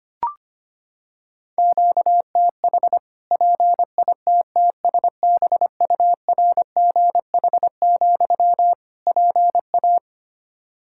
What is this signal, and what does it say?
Morse code, heard as a keyed tone, about 25 wpm, 700 Hz. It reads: QTH PITTSBURGH, PA
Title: Practice QSO - Exchange 1 - Beginning - 25wpm